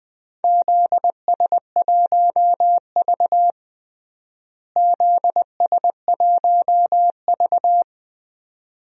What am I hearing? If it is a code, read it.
Morse code, heard as a keyed tone, about 21 wpm, 700 Hz. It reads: ZS1V ZS1V
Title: DXCC-100 Call Signs - 4 Characters - Repeated 2x - 20wpm